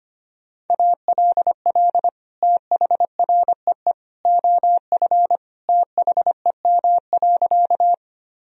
Morse code, heard as a keyed tone, about 25 wpm, 700 Hz. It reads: ALL THREE OF THEM.
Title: Sentences from Top 1000 Words - Review Part 1 - 25wpm